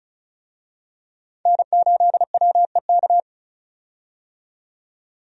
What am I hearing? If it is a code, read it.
Morse code, heard as a keyed tone, about 36 wpm, 700 Hz. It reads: N8WEK
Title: USA Call Signs - 5 Characters - 35wpm